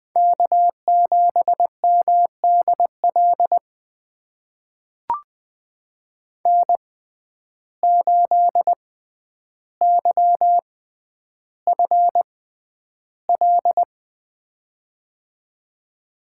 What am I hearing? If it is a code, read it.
Morse code, heard as a keyed tone, about 20 wpm, spaced as for 10 wpm, 700 Hz. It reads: K7MDL N8YFL
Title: USA Call Signs - 5 Characters - 6x Character Spacing - 20wpm